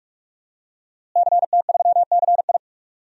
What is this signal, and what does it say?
Morse code, heard as a keyed tone, about 44 wpm, 700 Hz. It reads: CT3CI
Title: DXCC-100 Call Signs - 5 Characters - Repeated 3x - 45wpm